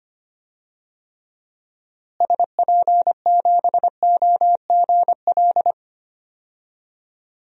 Morse code, heard as a keyed tone, about 25 wpm, 700 Hz. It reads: SP7OGL
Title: DXCC-100 Call Signs - 6 Characters - Repeated 2x - 25wpm